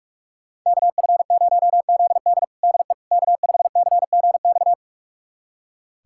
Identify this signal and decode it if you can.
Morse code, heard as a keyed tone, about 46 wpm, 700 Hz. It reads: KF0ZD DE K5CGX
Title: USA Callsign DE Callsign - 5 Characters - 45wpm